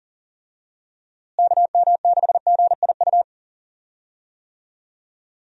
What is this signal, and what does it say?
Morse code, heard as a keyed tone, about 40 wpm, 700 Hz. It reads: KM6GIU
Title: USA Call Signs - 6 Characters - Repeated 2x - 40wpm